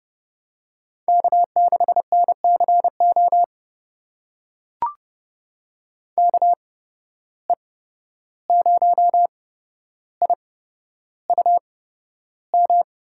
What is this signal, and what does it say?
Morse code, heard as a keyed tone, about 30 wpm, spaced as for 12 wpm, 700 Hz. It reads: K6NCO KE0IUM
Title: USA Call Signs - 8x Character Spacing - 30wpm